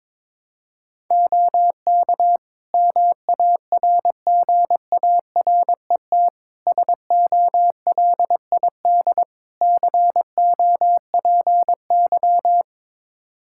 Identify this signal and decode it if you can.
Morse code, heard as a keyed tone, about 22 wpm, 700 Hz. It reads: OK MARGARET SOLID COPY
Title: Practice QSO - Exchange 3 - More Info - 22wpm